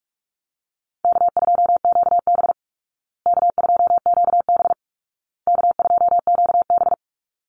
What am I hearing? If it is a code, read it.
Morse code, heard as a keyed tone, about 44 wpm, 700 Hz. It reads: K2QB K2QB K2QB
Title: Call Signs Repeated 2x - 45wpm